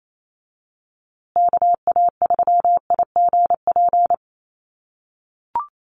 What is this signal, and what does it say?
Morse code, heard as a keyed tone, about 28 wpm, 700 Hz. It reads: KA3IGP
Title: USA Call Signs Repeated 2x 28wpm